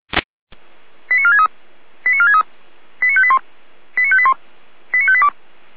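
Five-tone selective call (ZVEI), heard as quick runs of five tones, tones of about 80 ms. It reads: 89463 89463 89581 89581 89582
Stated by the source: ZVEI five-tone selective call received by SDR.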